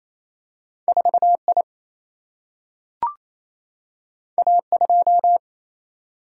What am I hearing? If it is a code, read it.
Morse code, heard as a keyed tone, about 28 wpm, 700 Hz. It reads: 4 I A 2
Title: Single-Letter Single-Number 28wpm